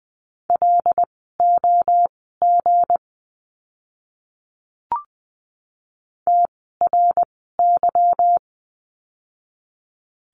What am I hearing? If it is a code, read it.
Morse code, heard as a keyed tone, about 20 wpm, spaced as for 15 wpm, 700 Hz. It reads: LOG TRY
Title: 160 Lesson 26f - Letter g - Words 20wpm